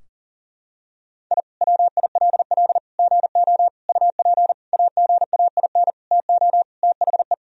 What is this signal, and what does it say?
Morse code, heard as a keyed tone, about 37 wpm, 700 Hz. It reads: I WILL GO UP AGAIN TO THE
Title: Sentences from Top 1000 Words - Review Part 3 - 40wpm